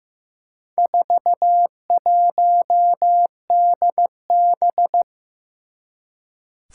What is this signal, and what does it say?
Morse code, heard as a keyed tone, about 15 wpm, 700 Hz. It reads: 41DB
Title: Four Challenging Letter-Number Combinations 15wpm